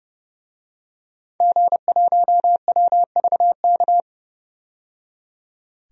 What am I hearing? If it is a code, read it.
Morse code, heard as a keyed tone, about 30 wpm, 700 Hz. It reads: G1WVK
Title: DXCC-100 Call Signs - 5 Characters - 30wpm